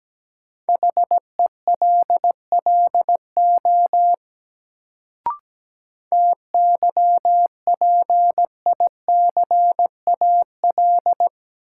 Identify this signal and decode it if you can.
Morse code, heard as a keyed tone, about 17 wpm, 700 Hz. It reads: HELLO TYPICAL
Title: Sentences from Top 1100 Words - New Word Drill - 17wpm